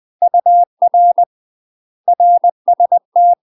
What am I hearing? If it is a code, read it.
Morse code, heard as a keyed tone, about 20 wpm, 700 Hz. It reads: UR RST
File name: MASTER QSOs TRAINING/20WPM